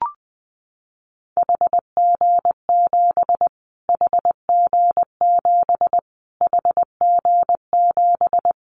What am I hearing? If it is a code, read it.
Morse code, heard as a keyed tone, about 20 wpm, 700 Hz. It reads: HG7 HG7 HG7
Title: Callsign Prefixes for Top 25 DXCC - Repeated 3x - 20wpm